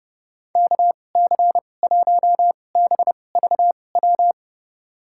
Morse code, heard as a keyed tone, about 30 wpm, spaced as for 22 wpm, 700 Hz. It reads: KC1BVW
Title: USA Call Signs - 6 Characters - 2x Character Spacing - 30wpm